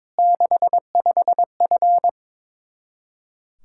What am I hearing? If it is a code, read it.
Morse code, heard as a keyed tone, about 22 wpm, 700 Hz. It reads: 65F
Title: Practice QSO - Exchange 2 - Info - 22wpm